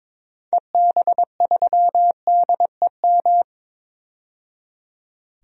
Morse code, heard as a keyed tone, about 22 wpm, 700 Hz. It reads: EB3DEM
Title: DXCC-100 Call Signs - 6 Characters - 22wpm